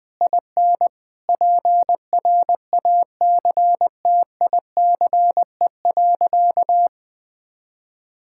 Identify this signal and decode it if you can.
Morse code, heard as a keyed tone, about 21 wpm, 700 Hz. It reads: IN PRACTICE.
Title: Sentences from Top 1000 Words - Review Part 5 - 20wpm